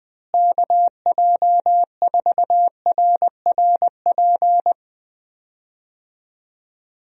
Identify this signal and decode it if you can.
Morse code, heard as a keyed tone, about 20 wpm, 700 Hz. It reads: KJ4RRP
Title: USA Call Signs - 6 Characters - Repeated 2x - 20wpm